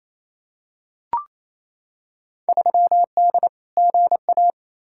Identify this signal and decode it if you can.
Morse code, heard as a keyed tone, about 28 wpm, 700 Hz. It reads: 3D GA
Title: Field Day Exchanges Encore - 28wpm